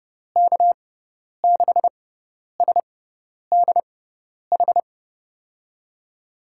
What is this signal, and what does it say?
Morse code, heard as a keyed tone, about 30 wpm, spaced as for 10 wpm, 700 Hz. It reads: K6SDH
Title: USA Call Signs - 6 Characters - 6x Character Spacing - 30wpm